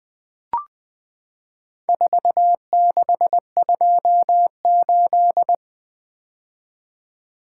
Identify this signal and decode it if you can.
Morse code, heard as a keyed tone, about 20 wpm, 700 Hz. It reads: 4628
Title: Four Letter-Number Combinations - 20wpm